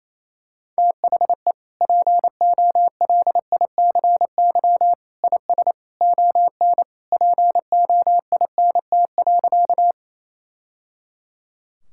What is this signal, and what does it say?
Morse code, heard as a keyed tone, about 28 wpm, 700 Hz. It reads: THE POLICY IS ON POINT.
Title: Sentences from Top 1000 Words - Review Part 5 - 28wpm